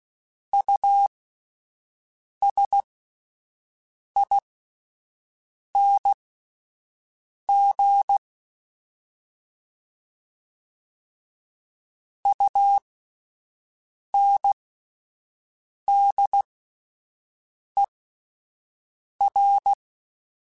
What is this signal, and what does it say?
Morse code, heard as a keyed tone, about 16 wpm, spaced as for 5 wpm, 800 Hz. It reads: USING UNDER
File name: Alphabetical / Alphabetical mp3s / Us @5